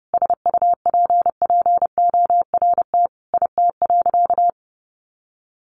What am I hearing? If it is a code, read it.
Morse code, heard as a keyed tone, about 30 wpm, 700 Hz. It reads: SUPPORT IT.
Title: Sentences from Top 1300 Words 30wpm